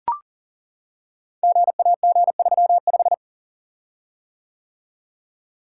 Morse code, heard as a keyed tone, about 42 wpm, 700 Hz. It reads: GAG35
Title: Code Groups - 5 Letter:Numbers - 40wpm